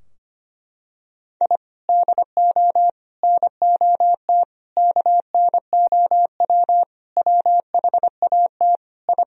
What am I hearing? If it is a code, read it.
Morse code, heard as a keyed tone, about 25 wpm, 700 Hz. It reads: I DO NOT KNOW WHAT I
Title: Sentences from Top 1000 Words - Review Part 3 - 25wpm